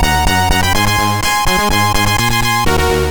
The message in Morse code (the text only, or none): none